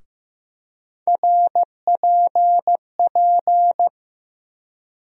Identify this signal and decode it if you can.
Morse code, heard as a keyed tone, about 15 wpm, 700 Hz. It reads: RPP
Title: USA Call Sign Postfixes - 3 Letters 15wpm